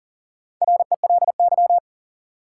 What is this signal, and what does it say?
Morse code, heard as a keyed tone, about 40 wpm, 700 Hz. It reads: RELY
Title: Sentences from Top 1200 Words - New Word Drill Alt - 40wpm